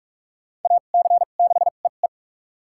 Morse code, heard as a keyed tone, about 43 wpm, spaced as for 32 wpm, 690 Hz. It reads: AC6EE
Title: USA Call Signs - 5 Characters - 2x Character Spacing - 45wpm